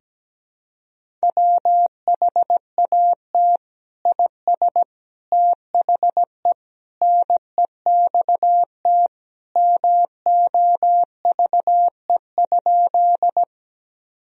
Morse code, heard as a keyed tone, about 17 wpm, 700 Hz. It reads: WHAT IS THE NEXT MOVE?
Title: Sentences from Top 1000 Words - Review Part 1 - 17wpm